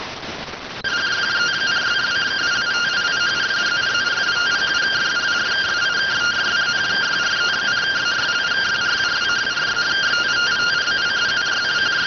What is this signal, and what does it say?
Two-tone teleprinter signal, about 170 Hz shift.